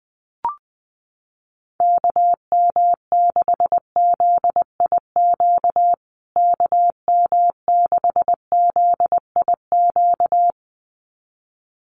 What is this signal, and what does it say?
Morse code, heard as a keyed tone, about 20 wpm, 700 Hz. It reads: KM6ZIQ KM6ZIQ
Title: USA Call Signs Repeated 2x 20wpm